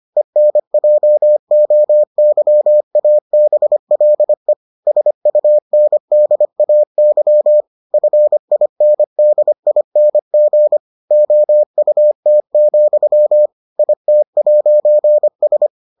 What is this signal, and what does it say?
Morse code, heard as a keyed tone, about 25 wpm, 600 Hz. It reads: ENJOYABLE SUNDAY FINDING OUT, IT'S